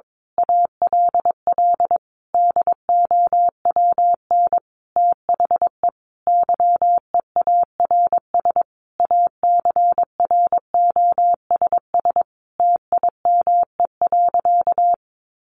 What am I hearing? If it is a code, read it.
Morse code, heard as a keyed tone, about 22 wpm, 700 Hz. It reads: ALL DOWN THE YEARS ACROSS TIME.
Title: A Sound Of Thunder 22wpm